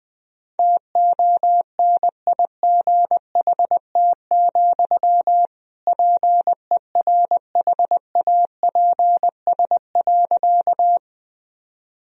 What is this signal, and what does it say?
Morse code, heard as a keyed tone, about 20 wpm, 700 Hz. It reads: TONIGHT, PERHAPS.
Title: Sentences from Top 1000 Words - Review Part 3 - 20wpm